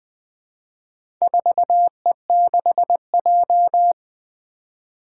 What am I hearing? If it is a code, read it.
Morse code, heard as a keyed tone, about 20 wpm, 700 Hz. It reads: 4E6J
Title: Four Challenging Letter-Number Combinations 20wpm